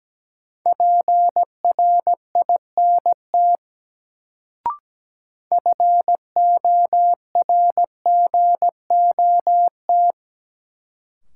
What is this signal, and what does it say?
Morse code, heard as a keyed tone, about 17 wpm, 700 Hz. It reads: PRINT FORGOT
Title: Sentences from Top 1400 Words - New Word Drill - 17wpm